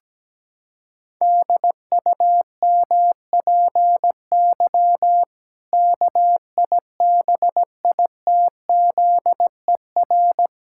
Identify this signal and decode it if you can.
Morse code, heard as a keyed tone, about 17 wpm, 700 Hz. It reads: DUMPY KIBITZER
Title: Pangram Challenge v2 - 17wpm